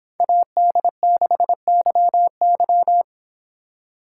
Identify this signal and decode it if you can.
Morse code, heard as a keyed tone, about 24 wpm, 700 Hz. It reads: AD6YY